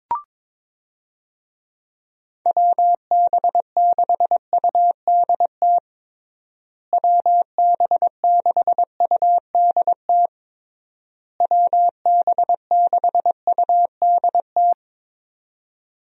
Morse code, heard as a keyed tone, about 22 wpm, 700 Hz. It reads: WB6UDT WB6UDT WB6UDT
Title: USA Call Signs - 6 Characters - Repeated 3x - 22wpm